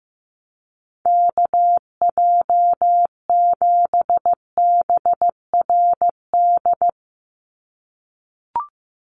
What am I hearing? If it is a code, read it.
Morse code, heard as a keyed tone, about 15 wpm, 700 Hz. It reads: KJ7BRD
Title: USA Call Signs 15wpm